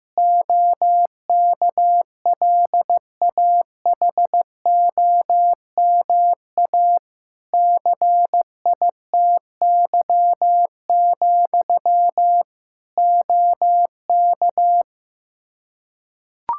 Morse code, heard as a keyed tone, about 15 wpm, 700 Hz. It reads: OKLAHOMA CITY, OK
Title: Practice QSO - Exchange 1 - Beginning - 15wpm